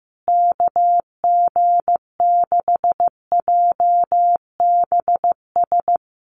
Morse code, heard as a keyed tone, about 15 wpm, 700 Hz. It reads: KG6JBS
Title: USA Call Signs Repeated 3x 15wpm